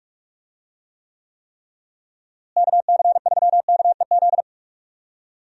Morse code, heard as a keyed tone, about 47 wpm, 700 Hz. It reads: KC3KEZ
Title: USA Call Signs - 6 Characters - Repeated 3x - 45wpm